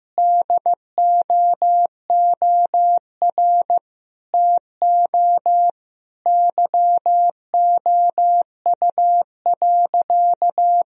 Morse code, heard as a keyed tone, about 15 wpm, 700 Hz. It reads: DOOR TO YOU.
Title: Sentences from Top 1000 Words - Review Part 1 - 15wpm